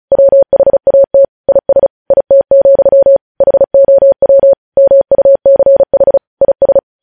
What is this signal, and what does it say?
Morse code, heard as a keyed tone, about 36 wpm, 550 Hz. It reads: WHAT IS IT, HOW MUCH IS
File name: podcast / News headlines-1x-35x35